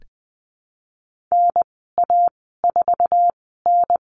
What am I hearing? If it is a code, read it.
Morse code, heard as a keyed tone, about 20 wpm, spaced as for 14 wpm, 700 Hz. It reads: NA4N
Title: 076 Lesson 13g - Letter L - Call Signs 20wpm